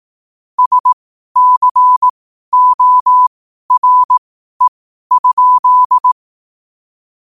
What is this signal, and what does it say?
Morse code, heard as a keyed tone, about 18 wpm, spaced as for 12 wpm, 1000 Hz. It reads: SCORE?
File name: qotdfiles / 12wpm